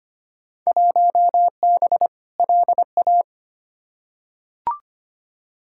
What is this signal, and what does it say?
Morse code, heard as a keyed tone, about 25 wpm, 700 Hz. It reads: 1B LA
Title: Field Day Exchanges - 25wpm